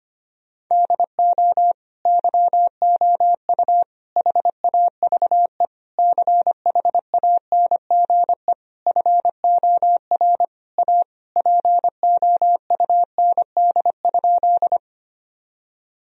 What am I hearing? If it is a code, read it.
Morse code, heard as a keyed tone, about 25 wpm, 700 Hz. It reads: DO YOU HAVE CHANGE FOR A POUND?
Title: Sentences from Top 1000 Words - Review Part 2 - 25wpm